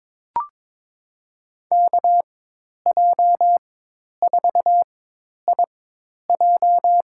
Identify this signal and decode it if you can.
Morse code, heard as a keyed tone, about 22 wpm, spaced as for 10 wpm, 700 Hz. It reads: KJ4IJ
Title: USA Call Signs - 5 Characters - 4x Character Spacing - 22wpm